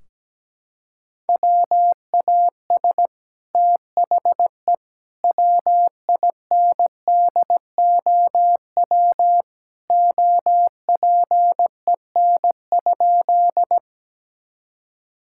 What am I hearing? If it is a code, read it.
Morse code, heard as a keyed tone, about 17 wpm, 700 Hz. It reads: WAS THE WINDOW OPEN?
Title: Sentences from Top 1600 Words 17wpm